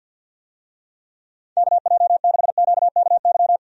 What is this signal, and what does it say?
Morse code, heard as a keyed tone, about 51 wpm, 700 Hz. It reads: KJ6QKY
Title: USA Call Signs - 6 Characters - 50wpm